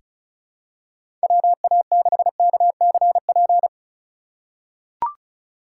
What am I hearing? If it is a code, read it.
Morse code, heard as a keyed tone, about 36 wpm, 700 Hz. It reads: WA6KCP
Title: USA Call Signs - 6 Characters - 2x Character Spacing - 35wpm